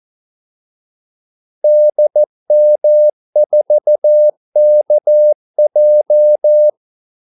Morse code, heard as a keyed tone, about 14 wpm, 600 Hz. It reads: DM4KJ
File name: Callsigns-14wpm